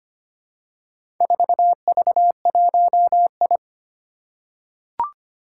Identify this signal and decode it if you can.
Morse code, heard as a keyed tone, about 25 wpm, 700 Hz. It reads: 4V1I
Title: Four Challenging Letter-Number Combinations 25wpm